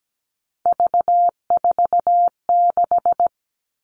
Morse code, heard as a keyed tone, about 17 wpm, 700 Hz. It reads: V46
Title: All Cut Numbers - 3-Digit 17wpm